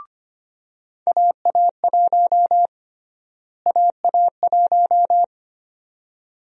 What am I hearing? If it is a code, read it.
Morse code, heard as a keyed tone, about 25 wpm, 700 Hz. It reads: AA1 AA1
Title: USA Call Sign Prefixes - Repeated 2x - 25wpm